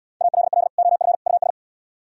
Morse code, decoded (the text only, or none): ALL OF US